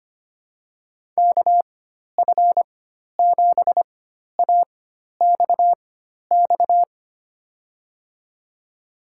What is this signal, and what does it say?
Morse code, heard as a keyed tone, about 25 wpm, spaced as for 12 wpm, 700 Hz. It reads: KF7AXX
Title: USA Call Signs - 6 Characters - 4x Character Spacing - 25wpm